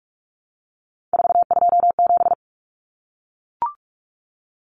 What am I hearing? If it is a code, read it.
Morse code, heard as a keyed tone, about 44 wpm, 700 Hz. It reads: 427
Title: Common Cut Numbers - 3-Digit 45wpm